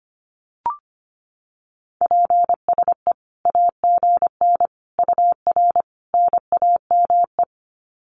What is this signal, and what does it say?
Morse code, heard as a keyed tone, about 25 wpm, 700 Hz. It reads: PSE AGN UR NAME
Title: Common QSO Phrases 25wpm